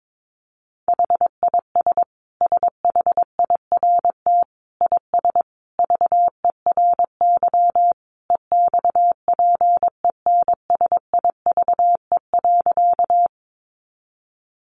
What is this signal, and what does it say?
Morse code, heard as a keyed tone, about 22 wpm, 700 Hz. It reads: HIS SHIRT IS VERY EXPENSIVE.
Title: Additional Easy Phrases 22wpm